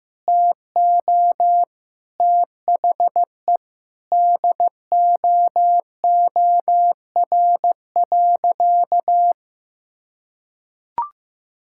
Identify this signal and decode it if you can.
Morse code, heard as a keyed tone, about 15 wpm, 700 Hz. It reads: TO THE DOOR.
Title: Sentences from Top 1600 Words 15wpm